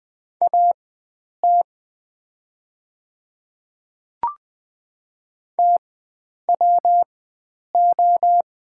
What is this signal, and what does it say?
Morse code, heard as a keyed tone, about 20 wpm, spaced as for 9 wpm, 690 Hz. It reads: AT TWO
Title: 4x Character Spacing - Top 100 Words - 20wpm